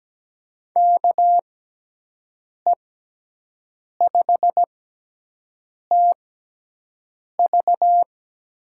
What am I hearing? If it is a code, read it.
Morse code, heard as a keyed tone, about 17 wpm, spaced as for 6 wpm, 700 Hz. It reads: KE5TV
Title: USA Call Signs - 6 Characters - 6x Character Spacing - 17wpm